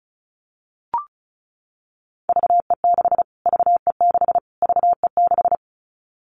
Morse code, heard as a keyed tone, about 34 wpm, 700 Hz. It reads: VE6 VE6 VE6
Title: Callsign Prefixes for Top 25 DXCC - Repeated 3x - 35wpm